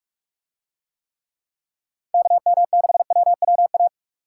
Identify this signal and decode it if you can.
Morse code, heard as a keyed tone, about 46 wpm, 700 Hz. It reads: KM6WWA